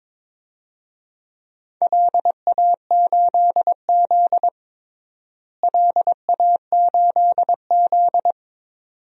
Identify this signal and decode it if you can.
Morse code, heard as a keyed tone, about 22 wpm, 700 Hz. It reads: LA8Z LA8Z
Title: DXCC-100 Call Signs - 4 Characters - Repeated 2x - 22wpm